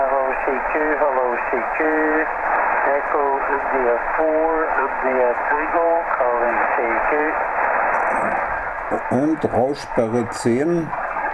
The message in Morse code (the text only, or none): none